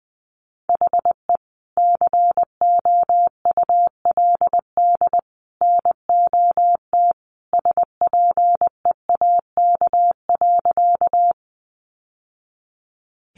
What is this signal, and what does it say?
Morse code, heard as a keyed tone, about 20 wpm, 700 Hz. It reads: HE COULD NOT SPEAK.
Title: Sentences from Top 1100 Words 20wpm